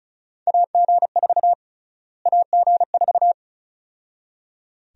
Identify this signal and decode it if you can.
Morse code, heard as a keyed tone, about 36 wpm, 700 Hz. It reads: AG4 AG4
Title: USA Call Sign Prefixes - Repeated 2x - 35wpm